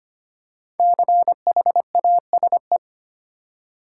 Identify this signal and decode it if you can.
Morse code, heard as a keyed tone, about 25 wpm, 700 Hz. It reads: CHASE
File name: 5 Letter Words - First 3 Letters Given - 25wpm